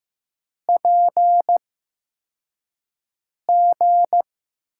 Morse code, heard as a keyed tone, about 15 wpm, spaced as for 4 wpm, 700 Hz. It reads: PG
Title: USA Call Signs - 8x Character Spacing - 15wpm